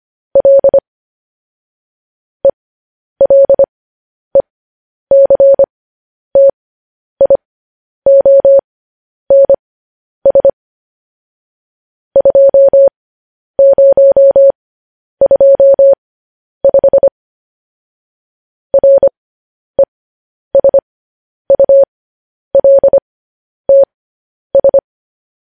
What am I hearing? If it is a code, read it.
Morse code, heard as a keyed tone, about 25 wpm, spaced as for 10 wpm, 550 Hz. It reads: L ELECTIONS 2025 RESULTS
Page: news-headlines-in-morse-code-at-10-wpm